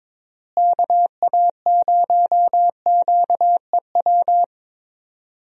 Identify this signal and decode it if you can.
Morse code, heard as a keyed tone, about 22 wpm, 700 Hz. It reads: KA0QEW
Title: USA Call Signs - 6 Characters - Repeated 2x - 22wpm